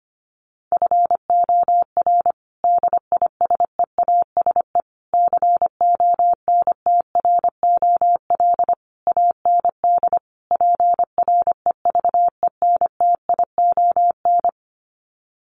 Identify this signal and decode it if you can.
Morse code, heard as a keyed tone, about 25 wpm, 700 Hz. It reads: FOR DISEASE CONTROL AND PREVENTION
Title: Sets of 5 Words Encore2 25wpm